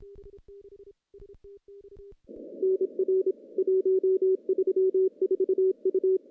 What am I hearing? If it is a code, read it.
Morse code, heard as a keyed tone, about 27 wpm, 380 Hz. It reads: NR 134 U
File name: contests / CW-SS-k5tr- -0100